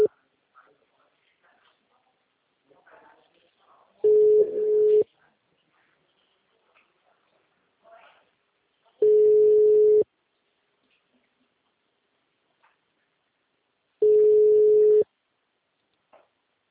Ringback tone (Europe, 425 Hz).